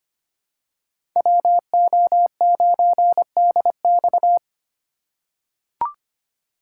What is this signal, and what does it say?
Morse code, heard as a keyed tone, about 25 wpm, 700 Hz. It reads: WO9DX
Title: USA Call Signs - 5 Characters - 3x Character Spacing - 25wpm